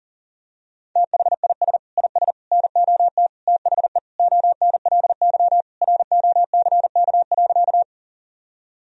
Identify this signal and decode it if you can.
Morse code, heard as a keyed tone, about 40 wpm, 700 Hz. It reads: THIS IS NOT THE ONLY ROCK.
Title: Sentences from Top 1000 Words - Review Part 2 - 40wpm